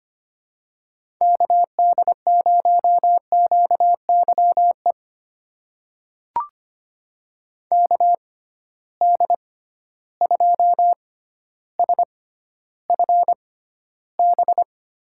Morse code, heard as a keyed tone, about 25 wpm, spaced as for 13 wpm, 700 Hz. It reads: KD0QYE KD2SFB
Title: USA Call Signs - 6 Characters - 6x Character Spacing - 25wpm